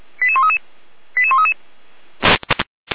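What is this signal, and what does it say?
Five-tone selective call (ZVEI), heard as quick runs of five tones, tones of about 70 ms. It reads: 88133 88133
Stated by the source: ZVEI five-tone selective call received with an SDR.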